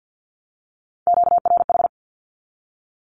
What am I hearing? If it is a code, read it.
Morse code, heard as a keyed tone, about 50 wpm, 700 Hz. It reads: QRH
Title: Q-Codes Repeated 3x 50wpm